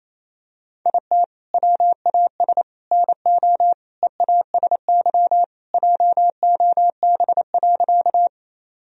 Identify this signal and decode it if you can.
Morse code, heard as a keyed tone, about 28 wpm, 700 Hz. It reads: IT WAS NO EASY JOB.
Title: Sentences from Top 1000 Words - Review Part 3 - 28wpm